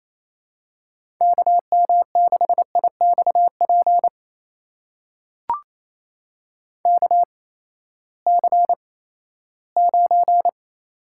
Morse code, heard as a keyed tone, about 28 wpm, spaced as for 16 wpm, 700 Hz. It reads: KM6IXP KC9
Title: USA Call Signs - 6 Characters - 8x Character Spacing - 28wpm